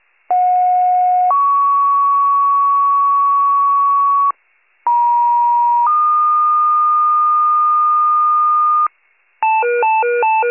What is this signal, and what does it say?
Two-tone page, heard as a one-second tone, then a three-second tone, about 730 then 1120 Hz, about 950 then 1220 Hz. It is heard from 0.3 to 8.9 s.
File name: pager / 56 RESCUE / wcalert